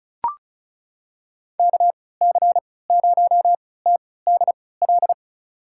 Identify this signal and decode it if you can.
Morse code, heard as a keyed tone, about 36 wpm, spaced as for 20 wpm, 700 Hz. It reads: KC0TDL
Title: USA Call Signs - 3x Character Spacing - 35wpm